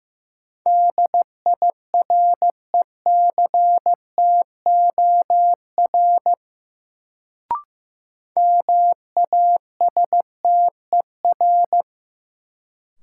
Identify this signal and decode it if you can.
Morse code, heard as a keyed tone, about 16 wpm, 700 Hz. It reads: DIRECTOR MASTER
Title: Sentences from Top 1300 Words - New Word Drill - 15wpm